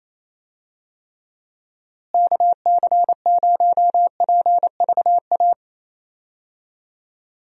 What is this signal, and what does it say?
Morse code, heard as a keyed tone, about 28 wpm, 700 Hz. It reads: KC0PVA
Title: USA Call Signs - 6 Characters - Repeated 2x - 28wpm